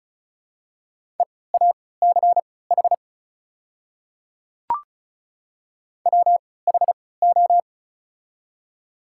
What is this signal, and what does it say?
Morse code, heard as a keyed tone, about 36 wpm, spaced as for 20 wpm, 700 Hz. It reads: EACH WHO
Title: 3x Character Spacing - Top 100 Words - 35wpm